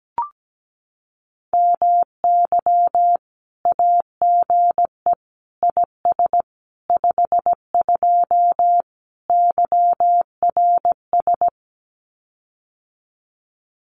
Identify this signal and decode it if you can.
Morse code, heard as a keyed tone, about 17 wpm, 700 Hz. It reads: MY AGE IS 52 YRS
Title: Common QSO Phrases Encore 17wpm